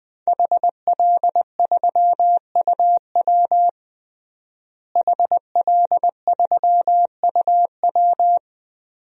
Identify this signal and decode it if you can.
Morse code, heard as a keyed tone, about 20 wpm, 700 Hz. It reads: HL3UW HL3UW
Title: DXCC-100 Call Signs - 5 Characters - Repeated 2x - 20wpm